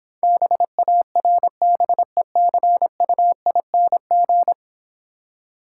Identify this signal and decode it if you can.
Morse code, heard as a keyed tone, about 26 wpm, 700 Hz. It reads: BARBECUING